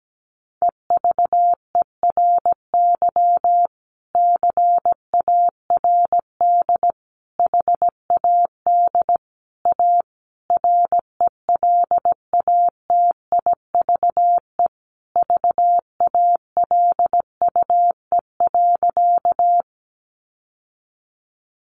Morse code, heard as a keyed tone, about 17 wpm, 700 Hz. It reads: EVERY CARD HAD A RELATIVE VALUE.
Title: Sentences from Top 1400 Words 17wpm